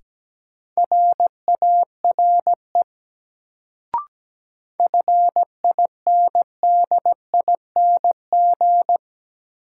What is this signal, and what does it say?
Morse code, heard as a keyed tone, about 17 wpm, 700 Hz. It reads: RARE FINDING
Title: Sentences from Top 1100 Words - New Word Drill Alt - 17wpm